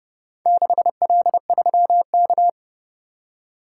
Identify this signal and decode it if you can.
Morse code, heard as a keyed tone, about 29 wpm, 700 Hz. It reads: 6L3K